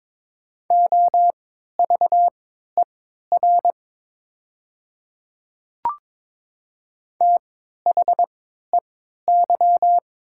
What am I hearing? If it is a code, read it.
Morse code, heard as a keyed tone, about 22 wpm, spaced as for 12 wpm, 700 Hz. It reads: OVER THEY
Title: 3x Character Spacing - Top 100 Words - 22wpm